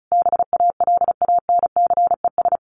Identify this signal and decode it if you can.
Morse code, heard as a keyed tone, about 34 wpm, 700 Hz. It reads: BALANCES
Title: A Sound Of Thunder 35wpm